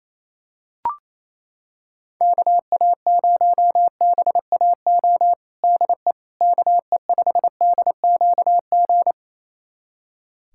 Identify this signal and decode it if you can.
Morse code, heard as a keyed tone, about 28 wpm, 700 Hz. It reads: KA0BAO DE KE5DQG
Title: USA Callsign DE Callsign 28wpm